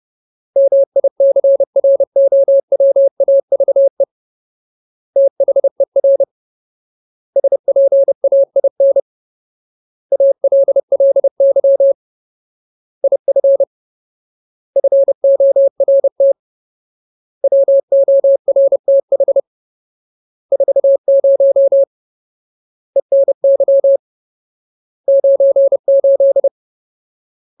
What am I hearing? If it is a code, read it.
Morse code, heard as a keyed tone, about 30 wpm, 550 Hz. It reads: MICROWAVE THER SPAIN ALLY IF FORT WORTH 40 ENY 98
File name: Ultimate Word List / 30WPM Ultimate